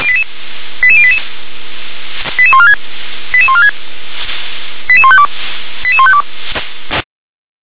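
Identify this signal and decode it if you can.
Five-tone selective call (ZVEI), heard as quick runs of five tones, tones of about 70 ms. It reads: 88088 88146 88146 88152 88152
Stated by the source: ZVEI five-tone selective call received on an SDR.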